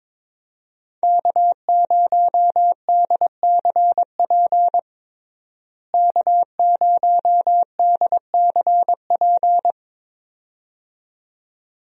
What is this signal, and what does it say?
Morse code, heard as a keyed tone, about 22 wpm, 700 Hz. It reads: K0DCP K0DCP
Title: USA Call Signs - 5 Characters - Repeated 2x - 22wpm